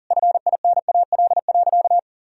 Morse code, heard as a keyed tone, about 37 wpm, 700 Hz. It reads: FINAL.
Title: Sentences from Top 1000 Words - Review Part 2 - 40wpm